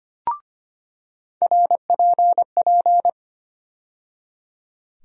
Morse code, heard as a keyed tone, about 25 wpm, 700 Hz. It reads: RPP
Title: USA Call Sign Postfixes - 3 Letters 25wpm